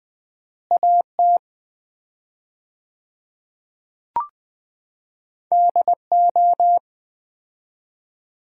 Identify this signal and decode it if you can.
Morse code, heard as a keyed tone, about 20 wpm, 700 Hz. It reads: AT DO
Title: 4x Word-Spacing - Sets of 4 Words - 20wpm